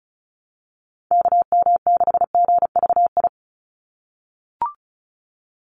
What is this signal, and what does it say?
Morse code, heard as a keyed tone, about 34 wpm, 700 Hz. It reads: KM6GVI
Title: USA Call Signs 35wpm